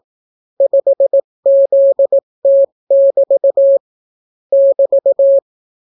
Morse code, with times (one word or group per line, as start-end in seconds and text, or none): 0.6-3.8 5ZT<BT>
4.5-5.4 <BT>